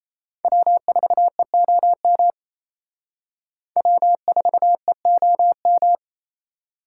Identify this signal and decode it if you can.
Morse code, heard as a keyed tone, about 35 wpm, 700 Hz. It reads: W4EOM W4EOM
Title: Speed Racing - USA 1x3 Call Signs 28wpm